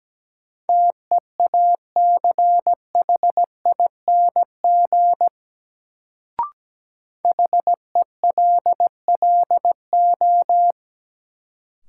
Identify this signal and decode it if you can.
Morse code, heard as a keyed tone, about 17 wpm, 700 Hz. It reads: TEACHING HELLO
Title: Sentences from Top 1100 Words - New Word Drill Alt - 17wpm